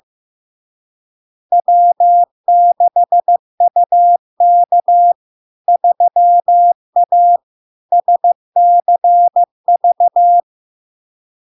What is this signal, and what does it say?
Morse code, heard as a keyed tone, about 15 wpm, 700 Hz. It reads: W6UK 3A SCV
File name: LARK-FD-2022-Log-in-CW-15wpm-700-Hz